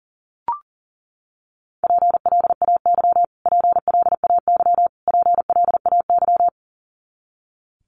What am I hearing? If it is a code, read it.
Morse code, heard as a keyed tone, about 40 wpm, 700 Hz. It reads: PLAY PLAY PLAY
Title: Top 200 Words Repeated 40wpm